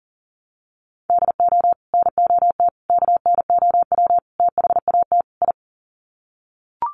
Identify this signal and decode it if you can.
Morse code, heard as a keyed tone, about 37 wpm, 700 Hz. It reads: DO NOT KNOW THAT I
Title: Sets of 5 Words Encore2 40wpm